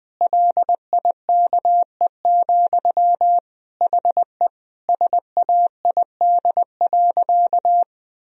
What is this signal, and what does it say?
Morse code, heard as a keyed tone, about 20 wpm, 700 Hz. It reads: LIKE, HE SAID.
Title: Sentences from Top 1000 Words - Review Part 2 - 20wpm